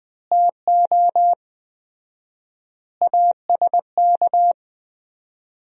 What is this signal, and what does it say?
Morse code, heard as a keyed tone, about 20 wpm, 700 Hz. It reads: TO ASK
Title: 4x Word-Spacing - Sentences from Top 200 Words - 20wpm